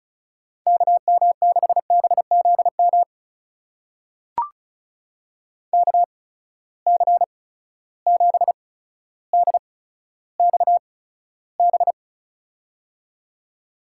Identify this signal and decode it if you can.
Morse code, heard as a keyed tone, about 36 wpm, spaced as for 15 wpm, 700 Hz. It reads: KM6BZM KC7DXB
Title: USA Call Signs - 6 Characters - 8x Character Spacing - 35wpm